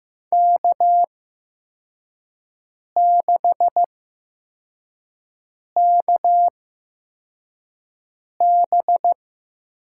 Morse code, heard as a keyed tone, about 15 wpm, spaced as for 4 wpm, 700 Hz. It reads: K6KB
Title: USA Call Signs - 4 Characters - 8x Character Spacing - 15wpm